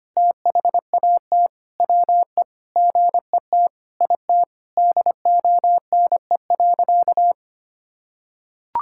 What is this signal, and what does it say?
Morse code, heard as a keyed tone, about 25 wpm, 700 Hz. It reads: THAT WE GET IT DONE.
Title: Sentences from Top 1600 Words 25wpm